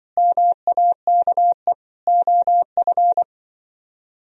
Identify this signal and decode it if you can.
Morse code, heard as a keyed tone, about 24 wpm, 700 Hz. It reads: MAKE OF
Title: Speed-Racing - Sets of 5 Words 20wpm